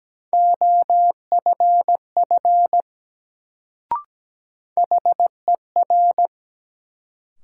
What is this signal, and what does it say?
Morse code, heard as a keyed tone, about 17 wpm, 700 Hz. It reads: OFF HER
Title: Sets of 2 Words - New Word Drill - 17wpm